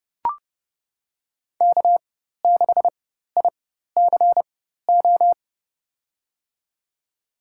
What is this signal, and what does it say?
Morse code, heard as a keyed tone, about 31 wpm, spaced as for 14 wpm, 700 Hz. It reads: K6ICO
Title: USA Call Signs - 5 Characters - 4x Character Spacing - 30wpm